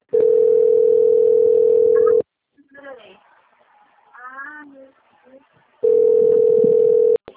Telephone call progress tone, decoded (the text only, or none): ringback tone